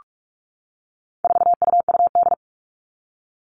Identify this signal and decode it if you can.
Morse code, heard as a keyed tone, about 44 wpm, 700 Hz. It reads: 4UUD
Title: All Cut Numbers - 4-Digit 45wpm